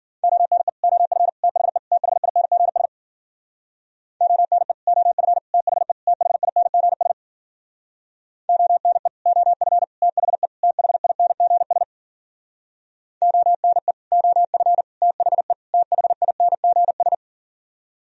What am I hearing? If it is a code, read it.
Morse code, heard as a keyed tone, about 52 wpm, 700 Hz. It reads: ONE OF THE THINGS ONE OF THE THINGS ONE OF THE THINGS ONE OF THE THINGS